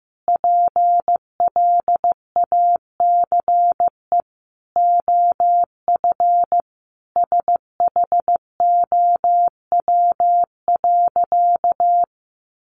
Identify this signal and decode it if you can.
Morse code, heard as a keyed tone, about 15 wpm, 700 Hz. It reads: PLACE OF SHOW.
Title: Sentences from Top 1400 Words 15wpm